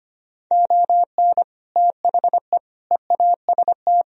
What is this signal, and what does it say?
Morse code, heard as a keyed tone, about 25 wpm, 700 Hz. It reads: ON THE EAST
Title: Sentences from Top 1000 Words - Review Part 4 - 25wpm